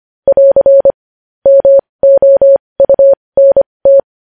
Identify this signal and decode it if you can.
Morse code, heard as a keyed tone, about 25 wpm, spaced as for 20 wpm, 550 Hz. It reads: <AR> MOUNT
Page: news-headlines-in-morse-code-at-20-wpm